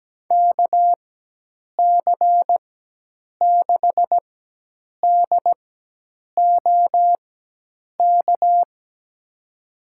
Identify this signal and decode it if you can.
Morse code, heard as a keyed tone, about 17 wpm, spaced as for 8 wpm, 700 Hz. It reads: KC6DOK
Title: USA Call Signs - 6 Characters - 4x Character Spacing - 17wpm